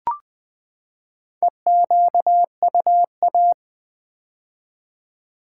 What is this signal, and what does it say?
Morse code, heard as a keyed tone, about 20 wpm, 700 Hz. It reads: EQUA
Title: Build A Word - 4 Letters At A Time - 8-Letter Words - 20wpm